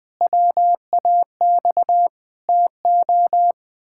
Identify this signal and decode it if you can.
Morse code, heard as a keyed tone, about 20 wpm, 700 Hz. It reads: WAX TO
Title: Pangram Challenge v2 - 20wpm